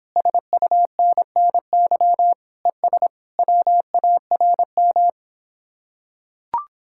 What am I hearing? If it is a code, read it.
Morse code, heard as a keyed tone, about 26 wpm, 700 Hz. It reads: SUNNY ES WARM